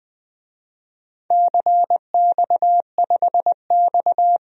Morse code, heard as a keyed tone, about 20 wpm, 700 Hz. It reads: CX5X
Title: DXCC-100 Call Signs - 4 Characters - Repeated 2x - 20wpm